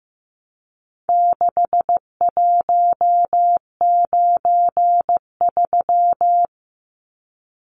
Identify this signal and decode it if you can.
Morse code, heard as a keyed tone, about 15 wpm, 700 Hz. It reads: 6193
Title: Numbers - 4-Digit 15wpm